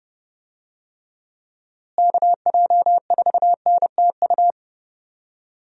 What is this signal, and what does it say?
Morse code, heard as a keyed tone, about 30 wpm, 700 Hz. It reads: KJ4NTU
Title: USA Call Signs - 6 Characters - Repeated 3x - 30wpm